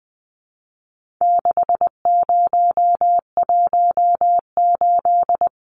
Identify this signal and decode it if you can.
Morse code, heard as a keyed tone, about 20 wpm, 700 Hz. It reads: 6018
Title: Numbers - 4-Digit 20wpm